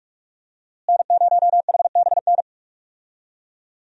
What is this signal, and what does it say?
Morse code, heard as a keyed tone, about 46 wpm, 700 Hz. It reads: N0HBN
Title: USA Call Signs - 5 Characters - Repeated 3x - 45wpm